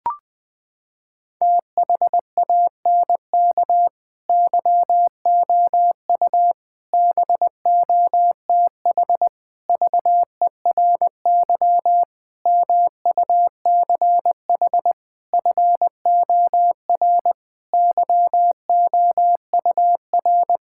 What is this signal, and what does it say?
Morse code, heard as a keyed tone, about 20 wpm, 700 Hz. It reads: THANK YOU BOTH VERY MUCH FOR YOUR
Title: Sentences with 5-Word Phrases - 20wpm